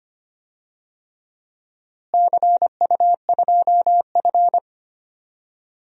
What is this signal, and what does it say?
Morse code, heard as a keyed tone, about 25 wpm, 700 Hz. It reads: CU2F
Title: DXCC-100 Call Signs - 4 Characters - Repeated 3x - 25wpm